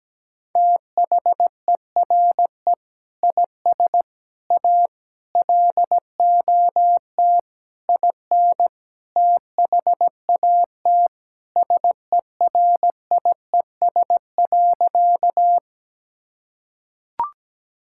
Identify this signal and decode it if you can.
Morse code, heard as a keyed tone, about 17 wpm, 700 Hz. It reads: THERE IS A LOT IN THAT SERIES.
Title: Sentences from Top 1000 Words - Review Part 5 - 17wpm